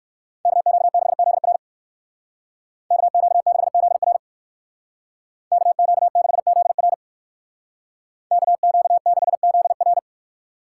Morse code, heard as a keyed tone, about 59 wpm, 700 Hz. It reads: KQ6ZR KQ6ZR KQ6ZR KQ6ZR